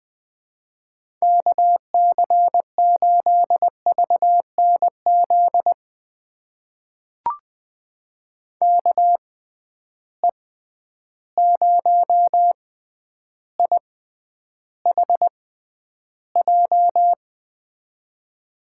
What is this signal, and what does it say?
Morse code, heard as a keyed tone, about 20 wpm, spaced as for 10 wpm, 700 Hz. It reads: KC8VNZ KE0IHJ
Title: USA Call Signs - 6 Characters - 6x Character Spacing - 20wpm